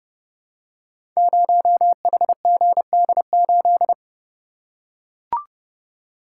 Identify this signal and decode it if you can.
Morse code, heard as a keyed tone, about 30 wpm, 700 Hz. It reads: 0HGD8
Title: Code Groups - 5 Letter:Numbers - 30wpm